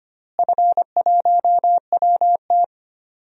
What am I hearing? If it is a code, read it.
Morse code, heard as a keyed tone, about 25 wpm, 700 Hz. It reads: F1WT
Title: DXCC-100 Call Signs - 4 Characters - Repeated 3x - 25wpm